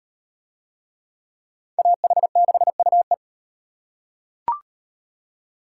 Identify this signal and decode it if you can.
Morse code, heard as a keyed tone, about 36 wpm, 700 Hz. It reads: AH6UE